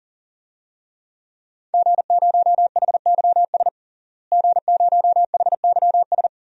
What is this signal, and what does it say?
Morse code, heard as a keyed tone, about 40 wpm, 700 Hz. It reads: G0HYS G0HYS
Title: DXCC-100 Call Signs - 5 Characters - Repeated 3x - 40wpm